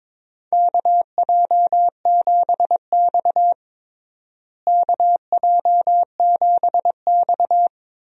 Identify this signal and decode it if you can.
Morse code, heard as a keyed tone, about 22 wpm, 700 Hz. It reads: KJ7X KJ7X
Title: USA Call Signs - 4 Characters - Repeated 3x - 22wpm